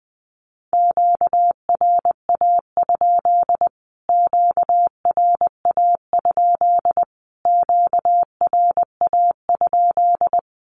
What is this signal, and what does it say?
Morse code, heard as a keyed tone, about 20 wpm, 700 Hz. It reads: QRA? QRA? QRA?
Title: Q-Codes Repeated 3x 20wpm